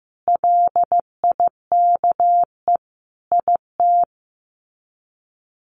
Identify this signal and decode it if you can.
Morse code, heard as a keyed tone, about 15 wpm, 700 Hz. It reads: LIKE IT
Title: Sets of 5 Words Encore 15wpm